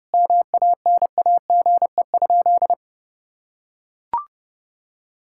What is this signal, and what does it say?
Morse code, heard as a keyed tone, about 31 wpm, 700 Hz. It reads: MANAGE?
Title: Sentences from Top 1600 Words 30wpm